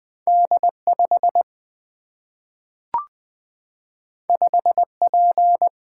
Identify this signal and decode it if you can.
Morse code, heard as a keyed tone, about 20 wpm, 700 Hz. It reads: D 5 5 P
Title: Single-Letter Single-Number 20wpm